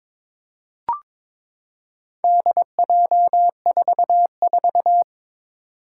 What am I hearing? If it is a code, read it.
Morse code, heard as a keyed tone, about 23 wpm, 700 Hz. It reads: DJ44
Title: Four Challenging Letter-Number Combinations 22wpm